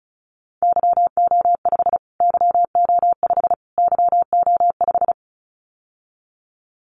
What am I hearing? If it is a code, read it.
Morse code, heard as a keyed tone, about 34 wpm, 700 Hz. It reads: YO5 YO5 YO5
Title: Callsign Prefixes for Top 25 DXCC - Repeated 3x - 35wpm